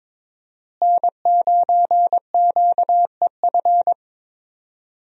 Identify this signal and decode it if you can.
Morse code, heard as a keyed tone, about 22 wpm, 700 Hz. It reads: N9QEF
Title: USA Call Signs - 5 Characters - Repeated 2x - 22wpm